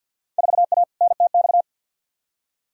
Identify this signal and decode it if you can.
Morse code, heard as a keyed tone, about 49 wpm, 700 Hz. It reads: 4A NTX
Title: Field Day Exchanges - 50wpm